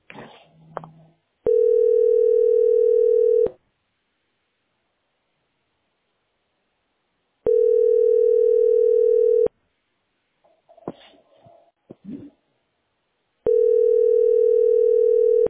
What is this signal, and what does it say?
Ringback tone (North America), 440 + 480 Hz.